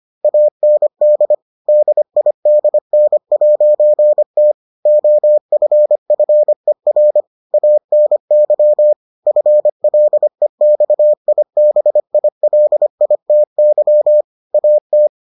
Morse code, heard as a keyed tone, about 25 wpm, 600 Hz. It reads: AND DIDN'T OFFER ANY FLEXIBILITY AT